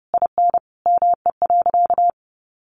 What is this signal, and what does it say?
Morse code, heard as a keyed tone, about 30 wpm, 700 Hz. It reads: IN ME.
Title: Sentences from Top 1500 Words 30wpm